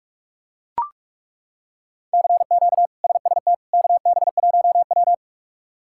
Morse code, heard as a keyed tone, about 42 wpm, 700 Hz. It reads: CQ SST KB1W
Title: K1USN SST QSO Practice - 45wpm